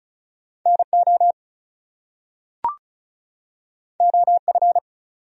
Morse code, heard as a keyed tone, about 36 wpm, 700 Hz. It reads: NO OF
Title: Top 100 Words 35wpm